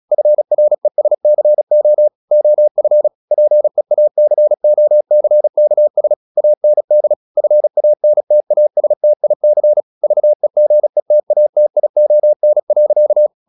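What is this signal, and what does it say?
Morse code, heard as a keyed tone, about 36 wpm, 600 Hz. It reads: FRESCO OF PEACOCKS AND FANTASTIC VEGETATION.
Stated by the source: Morse practice at 36 wpm, text of Ken Follett's novel Never, from CW med Gnister.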